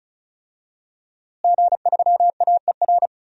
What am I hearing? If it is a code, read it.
Morse code, heard as a keyed tone, about 34 wpm, 700 Hz. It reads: G3AER
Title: DXCC-100 Call Signs - 5 Characters - Repeated 3x - 35wpm